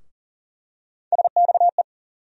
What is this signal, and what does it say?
Morse code, heard as a keyed tone, about 40 wpm, 700 Hz. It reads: SXE
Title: USA Call Sign Postfixes - 3 Letters 40wpm